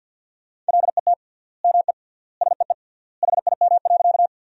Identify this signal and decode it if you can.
Morse code, heard as a keyed tone, about 50 wpm, 700 Hz. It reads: LET ME SEE HIM.
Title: 3x Word-Spacing - Sentences from Top 300 Words - 50wpm